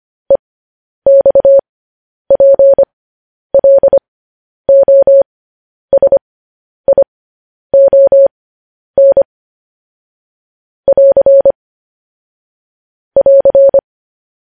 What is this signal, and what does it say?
Morse code, heard as a keyed tone, about 25 wpm, spaced as for 10 wpm, 550 Hz. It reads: EXPLOSION <AR> <AR>
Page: News Headlines in Morse Code at 10 WPM